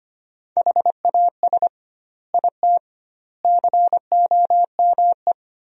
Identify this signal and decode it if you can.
Morse code, heard as a keyed tone, about 25 wpm, 700 Hz. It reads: HAS IT COME
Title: 2x Word-Spacing - Sentences from Top 100 Words - 25wpm